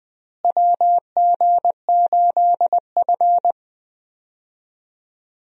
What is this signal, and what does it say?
Morse code, heard as a keyed tone, about 20 wpm, 700 Hz. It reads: WG8F
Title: USA Call Signs - 4 Characters - Repeated 2x - 20wpm